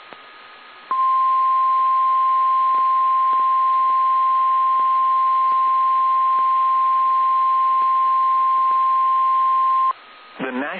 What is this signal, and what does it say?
Emergency Alert System (SAME) tones (the 1050 Hz attention tone), with speech. It is heard from 1.0 to 9.7 s.
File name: eas / KSDZKDJL / KSDZKDJL